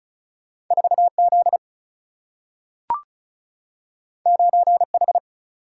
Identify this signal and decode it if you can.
Morse code, heard as a keyed tone, about 36 wpm, 700 Hz. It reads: 4 Z 9 H
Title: Single-Letter Single-Number 35wpm